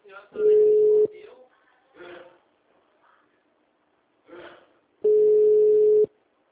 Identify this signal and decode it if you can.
Ringback tone (Europe, 425 Hz), with speech.
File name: out- -2049- - - .27218